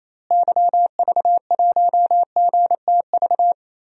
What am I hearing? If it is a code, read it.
Morse code, heard as a keyed tone, about 27 wpm, 700 Hz. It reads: YV1GTV
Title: DXCC-100 Call Signs - 6 Characters - Repeated 2x - 28wpm